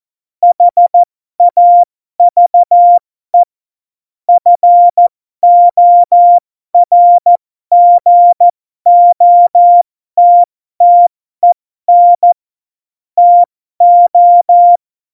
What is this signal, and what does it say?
Morse code, heard as a keyed tone, about 14 wpm, spaced as for 12 wpm, 700 Hz. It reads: HAVE FORGOTTEN TO